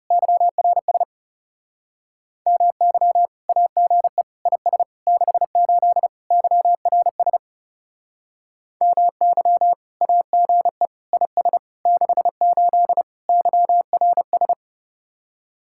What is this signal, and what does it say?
Morse code, heard as a keyed tone, about 35 wpm, 700 Hz. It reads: YRS MY AGE IS 68 YRS MY AGE IS 68 YRS
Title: Speed-Racing Common QSO Phrases Encore - 30wpm